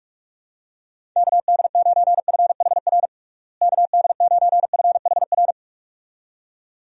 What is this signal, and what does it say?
Morse code, heard as a keyed tone, about 46 wpm, 700 Hz. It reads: KD9FHR KD9FHR
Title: USA Call Signs - 6 Characters - Repeated 2x - 45wpm